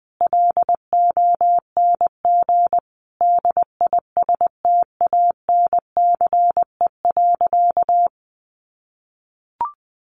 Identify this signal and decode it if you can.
Morse code, heard as a keyed tone, about 20 wpm, 700 Hz. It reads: LONG DISTANCE.
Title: Sentences from Top 1200 Words 20wpm